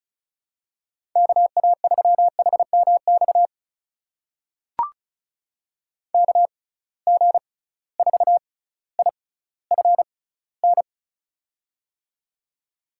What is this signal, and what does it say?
Morse code, heard as a keyed tone, about 34 wpm, spaced as for 18 wpm, 700 Hz. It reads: KA3HMX KG4IFN
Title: USA Call Signs - 6 Characters - 6x Character Spacing - 35wpm